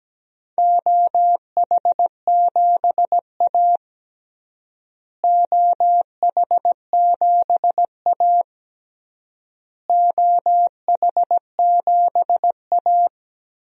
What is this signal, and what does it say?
Morse code, heard as a keyed tone, about 17 wpm, 700 Hz. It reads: OH7A OH7A OH7A
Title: DXCC-100 Call Signs - 4 Characters - Repeated 3x - 17wpm